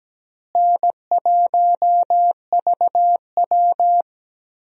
Morse code, heard as a keyed tone, about 17 wpm, 700 Hz. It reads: N1VW
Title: USA Call Signs - 4 Characters - Repeated 2x - 17wpm